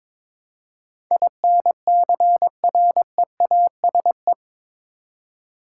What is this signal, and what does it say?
Morse code, heard as a keyed tone, about 22 wpm, 700 Hz. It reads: INCREASE
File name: practice / 22wpm / Eight Letter Words - 22wpm